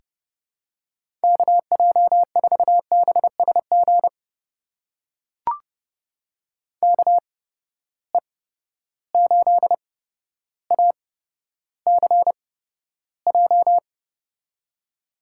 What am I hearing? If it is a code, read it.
Morse code, heard as a keyed tone, about 30 wpm, spaced as for 13 wpm, 700 Hz. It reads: KJ4BSG KE8ACJ
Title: USA Call Signs - 6 Characters - 8x Character Spacing - 30wpm